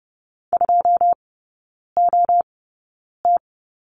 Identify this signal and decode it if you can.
Morse code, heard as a keyed tone, about 30 wpm, spaced as for 9 wpm, 700 Hz. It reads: 2OT
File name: 081 Lesson 14e - Number 2 - Three Characters 30wpm